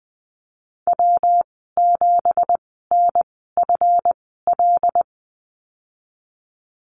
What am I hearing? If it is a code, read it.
Morse code, heard as a keyed tone, about 19 wpm, spaced as for 14 wpm, 700 Hz. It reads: W7NFL
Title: 193 Lesson 31g - Letter v - Call Signs 20wpm